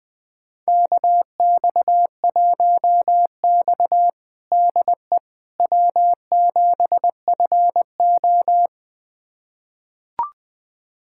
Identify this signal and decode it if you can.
Morse code, heard as a keyed tone, about 20 wpm, 700 Hz. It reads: KX1X DE W7FO
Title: USA Callsign DE Callsign - 4 Characters - 20wpm